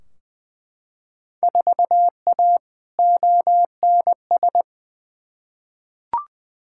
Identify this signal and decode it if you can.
Morse code, heard as a keyed tone, about 20 wpm, 700 Hz. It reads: 4A ONS
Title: Field Day Exchanges - 20wpm